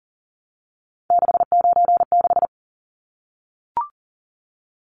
Morse code, heard as a keyed tone, about 40 wpm, 700 Hz. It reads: 696
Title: Numbers - 3-Digit 40wpm